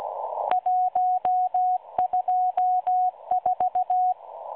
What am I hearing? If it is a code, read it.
Morse code, heard as a keyed tone, about 16 wpm, 730 Hz. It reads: 124
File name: M12 - 8116 kHz CW 2-11-14